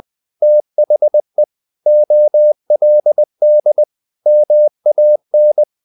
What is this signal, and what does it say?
Morse code, heard as a keyed tone, about 20 wpm, 600 Hz. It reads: THE OLD MAN
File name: the old man and the sea part 2 20wpm